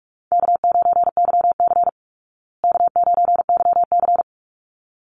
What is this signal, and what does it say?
Morse code, heard as a keyed tone, about 44 wpm, 700 Hz. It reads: K9YC K9YC
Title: Call Signs Repeated 3x - 45wpm